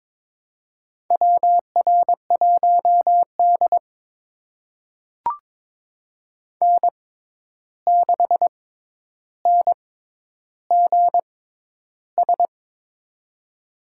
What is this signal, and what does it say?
Morse code, heard as a keyed tone, about 22 wpm, spaced as for 11 wpm, 700 Hz. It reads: WR1D N6NGS
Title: USA Call Signs - 6x Character Spacing - 22wpm